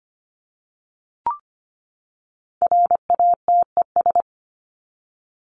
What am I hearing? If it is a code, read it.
Morse code, heard as a keyed tone, about 25 wpm, 700 Hz. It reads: RATES
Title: ES:S Suffix - Top 100 Words with Base Words 25wpm